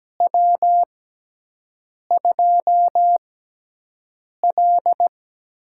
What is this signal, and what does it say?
Morse code, heard as a keyed tone, about 17 wpm, spaced as for 6 wpm, 700 Hz. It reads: W2L
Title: USA Call Signs - 6x Character Spacing - 17wpm